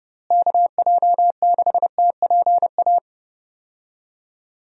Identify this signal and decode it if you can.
Morse code, heard as a keyed tone, about 30 wpm, 700 Hz. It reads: KJ6TPA
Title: USA Call Signs - 6 Characters - Repeated 2x - 30wpm